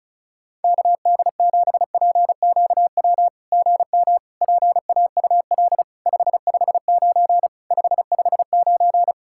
Morse code, heard as a keyed tone, about 36 wpm, 700 Hz. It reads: KD7PQW GM PAUL 559 559
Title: SOTA QSO Practice 35wpm